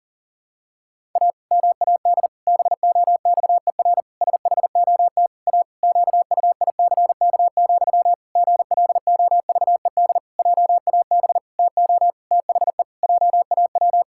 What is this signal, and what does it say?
Morse code, heard as a keyed tone, about 37 wpm, 700 Hz. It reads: A MAD BOXER SHOT A QUICK, GLOVED JAB TO THE JAW
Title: Pangram Challenge v2 - 40wpm